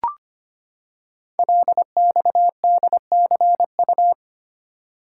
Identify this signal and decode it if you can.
Morse code, heard as a keyed tone, about 25 wpm, 700 Hz. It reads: LXDCU
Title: Code Groups - 5 Letters - 25wpm